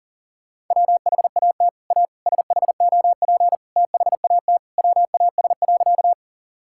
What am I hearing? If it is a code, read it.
Morse code, heard as a keyed tone, about 40 wpm, 700 Hz. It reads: WHAT A SHOP THAT WAS.
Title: Sentences from Top 1600 Words 40wpm